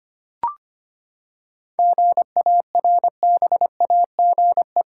Morse code, heard as a keyed tone, about 25 wpm, 700 Hz. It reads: GARBAGE
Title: Seven Letter Words - 25wpm